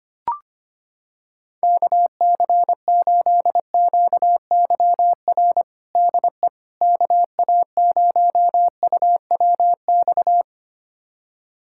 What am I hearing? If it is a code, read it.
Morse code, heard as a keyed tone, about 25 wpm, 700 Hz. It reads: KC8QYR DE KA0UWX
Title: USA Callsign DE Callsign - 6 Characters - 25wpm